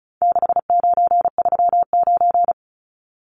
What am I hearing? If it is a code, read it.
Morse code, heard as a keyed tone, about 34 wpm, 700 Hz. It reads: 6939
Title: Numbers - 4-Digit 35wpm